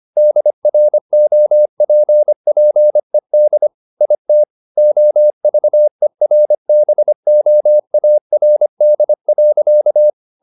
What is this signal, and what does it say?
Morse code, heard as a keyed tone, about 25 wpm, 600 Hz. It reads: DROPPED IT OVERBOARD.
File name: the old man and the sea part 1 25wpm